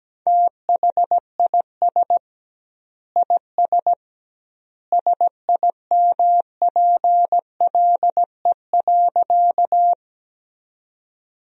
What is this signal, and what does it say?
Morse code, heard as a keyed tone, about 17 wpm, 700 Hz. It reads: THIS IS SIMPLE.
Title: 2x Word-Spacing - Sentences from Top 400 Words - 17wpm